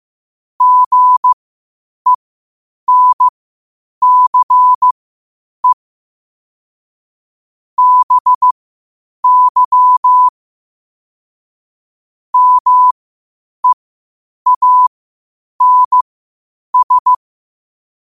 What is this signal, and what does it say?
Morse code, heard as a keyed tone, about 15 wpm, spaced as for 8 wpm, 1000 Hz. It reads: GENCE BY MEANS
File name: qotdfiles / 8wpm